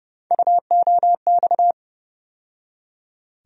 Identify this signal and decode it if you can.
Morse code, heard as a keyed tone, about 30 wpm, 700 Hz. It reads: UOX
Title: USA Call Sign Postfixes - 3 Letters 30wpm